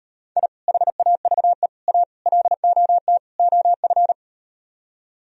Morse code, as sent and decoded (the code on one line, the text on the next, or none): .. / .... .- ...- . / .- / .-.. --- - / --- ..-.
I HAVE A LOT OF